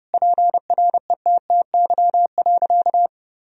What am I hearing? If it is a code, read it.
Morse code, heard as a keyed tone, about 32 wpm, 700 Hz. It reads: PRETTY.
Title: Sentences from Top 1000 Words - Review Part 1 - 30wpm